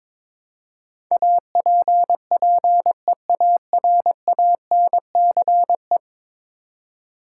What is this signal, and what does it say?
Morse code, heard as a keyed tone, about 21 wpm, 700 Hz. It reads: APPEARANCE
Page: morse-code-ninja-practice-22wpm